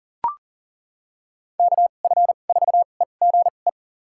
Morse code, heard as a keyed tone, about 40 wpm, spaced as for 29 wpm, 700 Hz. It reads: KF4EGE
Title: USA Call Signs - 6 Characters - 2x Character Spacing - 40wpm